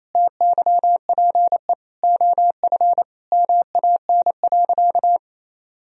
Morse code, heard as a keyed tone, about 28 wpm, 700 Hz. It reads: TYPE OF MAN.
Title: Sentences from Top 1000 Words - Review Part 3 - 28wpm